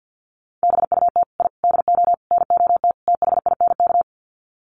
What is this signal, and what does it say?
Morse code, heard as a keyed tone, about 50 wpm, 700 Hz. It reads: BUT I DO NOT THINK
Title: Sets of 5 Words 50wpm